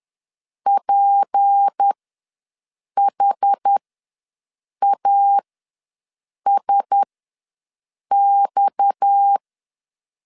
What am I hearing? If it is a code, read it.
Morse code, heard as a keyed tone, about 10 wpm, spaced as for 6 wpm, 800 Hz. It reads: PHASX
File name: LEZIONI / 6 PKX0 / 6 PKX0 ws 10wpm 4220